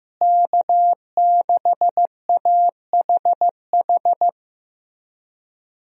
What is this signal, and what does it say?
Morse code, heard as a keyed tone, about 15 wpm, 700 Hz. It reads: K6AHH
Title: USA Call Signs - 5 Characters - Repeated 3x - 15wpm